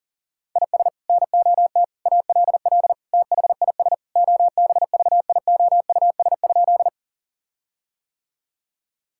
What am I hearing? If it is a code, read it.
Morse code, heard as a keyed tone, about 37 wpm, 700 Hz. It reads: IS NOT ALL THIS OBVIOUS?
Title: Sentences from Top 1600 Words 40wpm